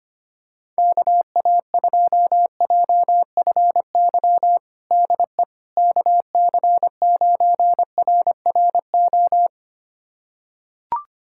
Morse code, heard as a keyed tone, about 25 wpm, 700 Hz. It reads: KA2JFY DE KC9RRO
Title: USA Callsign DE Callsign - 6 Characters - 25wpm